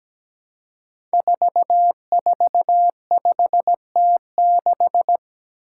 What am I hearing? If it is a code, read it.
Morse code, heard as a keyed tone, about 17 wpm, 700 Hz. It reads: 445T6
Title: Five Challenging Letter-Number Combinations 17wpm